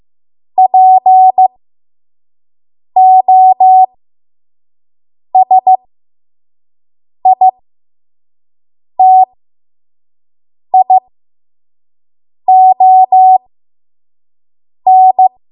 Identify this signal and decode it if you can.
Morse code, heard as a keyed tone, about 15 wpm, spaced as for 5 wpm, 750 Hz. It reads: POSITION
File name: Morse / 05WPM